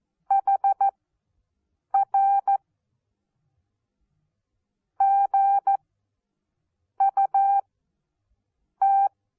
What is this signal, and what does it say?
Morse code, heard as a keyed tone, about 14 wpm, spaced as for 6 wpm, 790 Hz. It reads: HR GUT
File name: Onlinelehrgang / morsen / DJ4UF Morsen14